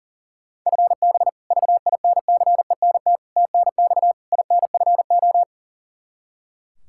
Morse code, heard as a keyed tone, about 40 wpm, 700 Hz. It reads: FB VINCENT TNX INFO
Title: Practice QSO - Exchange 3 - More Info - 40wpm